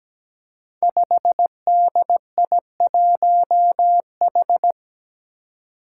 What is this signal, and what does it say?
Morse code, heard as a keyed tone, about 17 wpm, 700 Hz. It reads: 5DI1H
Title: Five Challenging Letter-Number Combinations 17wpm